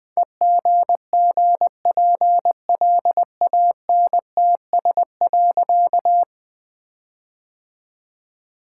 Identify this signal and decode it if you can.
Morse code, heard as a keyed tone, about 20 wpm, 700 Hz. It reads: EGGPLANTS.
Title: All-Age Appropriate Jokes - 20wpm